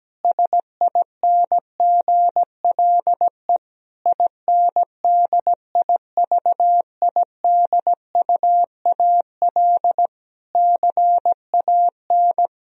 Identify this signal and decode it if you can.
Morse code, heard as a keyed tone, about 17 wpm, 700 Hz. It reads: SINGLE INDIVIDUAL CAN
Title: Sentences from Top 1000 Words - Review Part 4 - 17wpm